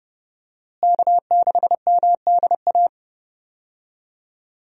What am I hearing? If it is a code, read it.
Morse code, heard as a keyed tone, about 32 wpm, 700 Hz. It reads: K6MDA
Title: USA Call Signs - 5 Characters - Repeated 3x - 30wpm